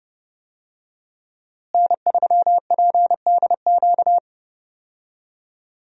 Morse code, heard as a keyed tone, about 30 wpm, 700 Hz. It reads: N3PDQ
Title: USA Call Signs - 5 Characters - Repeated 2x - 30wpm